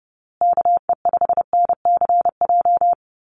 Morse code, heard as a keyed tone, about 30 wpm, 700 Hz. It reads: KE5NCJ
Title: USA Call Signs Repeated 3x 30wpm